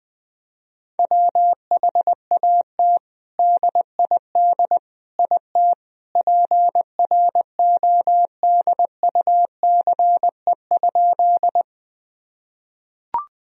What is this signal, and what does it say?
Morse code, heard as a keyed tone, about 20 wpm, 700 Hz. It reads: WHAT DID IT PRODUCE?
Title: Sentences from Top 1000 Words - Review Part 3 - 20wpm